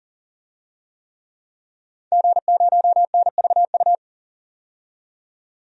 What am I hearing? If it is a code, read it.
Morse code, heard as a keyed tone, about 40 wpm, 700 Hz. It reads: G0NVU
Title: DXCC-100 Call Signs - 5 Characters - 40wpm